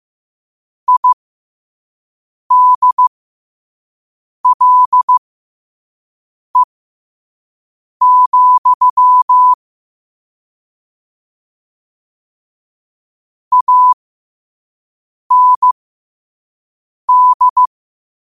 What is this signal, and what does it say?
Morse code, heard as a keyed tone, about 15 wpm, spaced as for 5 wpm, 1000 Hz. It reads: IDLE, AND
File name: qotdfiles / 5wpm